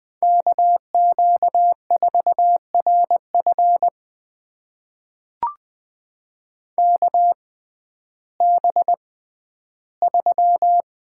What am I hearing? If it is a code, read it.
Morse code, heard as a keyed tone, about 20 wpm, spaced as for 12 wpm, 700 Hz. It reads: KQ4RF KB3
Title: USA Call Signs - 5 Characters - 6x Character Spacing - 20wpm